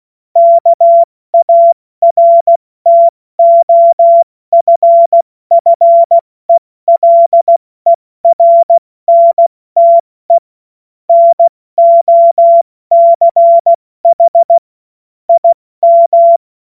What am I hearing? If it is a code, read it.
Morse code, heard as a keyed tone, about 16 wpm, 680 Hz. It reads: KARTOFFELERNTE NOCH IM
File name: Witz-16-September-2025-70-F80